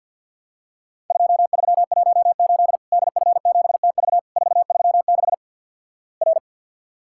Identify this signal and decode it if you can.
Morse code, heard as a keyed tone, about 50 wpm, 700 Hz. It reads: 2318 DF7TV 436 R
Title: Worked All Europe DX Contest - QTC Exchange - 50wpm